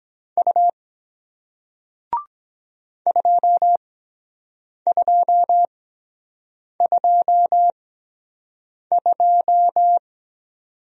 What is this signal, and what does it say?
Morse code, heard as a keyed tone, about 23 wpm, spaced as for 7 wpm, 700 Hz. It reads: U 2222